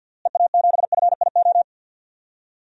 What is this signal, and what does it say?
Morse code, heard as a keyed tone, about 52 wpm, 700 Hz. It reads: EA7FIO
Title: DXCC-100 Call Signs - 6 Characters - Repeated 2x - 50wpm